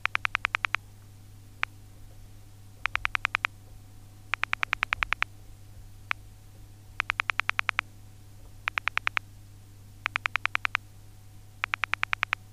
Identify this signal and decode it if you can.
Pulse dialling.
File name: telephone / libsounds.com - Pulse dial sounds in the handset 3